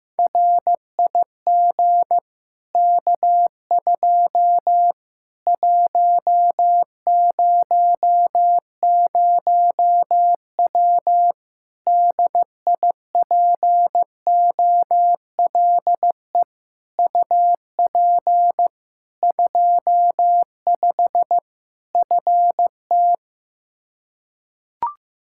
Morse code, heard as a keyed tone, about 15 wpm, 700 Hz. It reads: RIG K2 100W DIPOLE UP 25 FT
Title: Practice QSO - Exchange 2 - Info - 15wpm